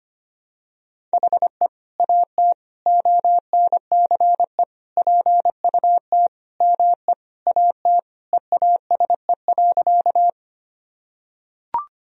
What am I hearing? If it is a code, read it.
Morse code, heard as a keyed tone, about 25 wpm, 700 Hz. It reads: HE AT ONCE PUT ME AT EASE.
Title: Sentences from Top 1000 Words - Review Part 1 - 25wpm